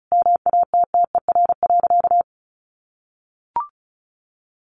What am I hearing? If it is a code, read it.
Morse code, heard as a keyed tone, about 34 wpm, 700 Hz. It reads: MATTER.
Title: Sentences from Top 1200 Words 35wpm